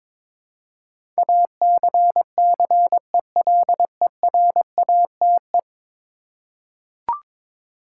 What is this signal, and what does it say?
Morse code, heard as a keyed tone, about 22 wpm, 700 Hz. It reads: ACCELERATE
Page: morse-code-ninja-practice-22wpm